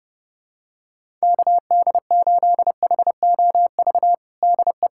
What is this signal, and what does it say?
Morse code, heard as a keyed tone, about 30 wpm, 700 Hz. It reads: KD8HOV DE
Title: USA Callsign DE Callsign - 6 Characters - 30wpm